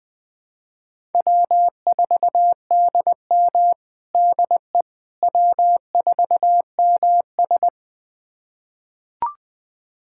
Morse code, heard as a keyed tone, about 20 wpm, 700 Hz. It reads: W4DM DE W4MS
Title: USA Callsign DE Callsign - 4 Characters - 20wpm